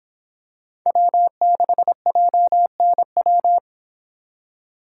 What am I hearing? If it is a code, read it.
Morse code, heard as a keyed tone, about 25 wpm, 700 Hz. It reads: W6JNW